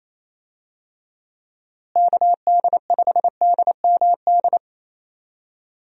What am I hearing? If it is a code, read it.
Morse code, heard as a keyed tone, about 28 wpm, 700 Hz. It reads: KD5DMD
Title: USA Call Signs - 6 Characters - Repeated 2x - 28wpm